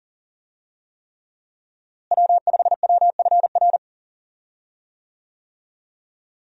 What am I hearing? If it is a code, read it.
Morse code, heard as a keyed tone, about 39 wpm, 700 Hz. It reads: W5WFR
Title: USA Call Signs - 5 Characters - Repeated 3x - 40wpm